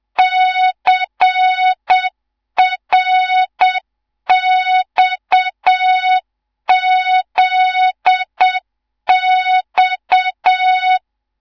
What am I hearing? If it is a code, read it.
Morse code, heard as a keyed tone, about 7 wpm, 3700 Hz, with noise. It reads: CRXZX
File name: Morze-HA2MN / hang / 18 IRASJELEK2